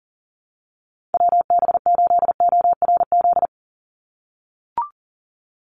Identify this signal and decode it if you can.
Morse code, heard as a keyed tone, about 40 wpm, 700 Hz. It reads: WB8ORZ
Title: USA Call Signs 40wpm